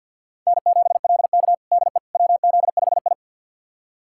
Morse code, heard as a keyed tone, about 49 wpm, 700 Hz. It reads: N7LK DE WZ5I
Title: USA Callsign DE Callsign - 4 Characters - 50wpm